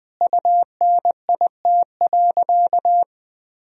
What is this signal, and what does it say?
Morse code, heard as a keyed tone, about 20 wpm, 700 Hz. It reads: UNIT.
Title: Sentences from Top 1000 Words - Review Part 2 - 20wpm